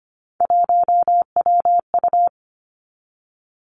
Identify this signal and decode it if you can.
Morse code, heard as a keyed tone, about 25 wpm, 700 Hz. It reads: 1WU
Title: Three Challenging Letter-Number Combinations 25wpm